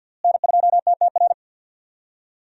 Morse code, heard as a keyed tone, about 52 wpm, 700 Hz. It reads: N2TTR
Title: USA Call Signs - 5 Characters - Repeated 2x - 50wpm